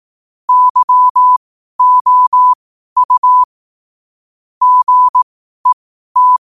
Morse code, heard as a keyed tone, about 18 wpm, spaced as for 12 wpm, 1000 Hz. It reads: YOU GET
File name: qotdfiles / 12wpm